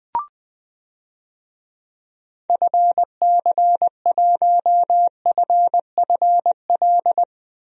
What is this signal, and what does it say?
Morse code, heard as a keyed tone, about 20 wpm, 700 Hz. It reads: FC1FFL
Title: DXCC-100 Call Signs - 6 Characters - 20wpm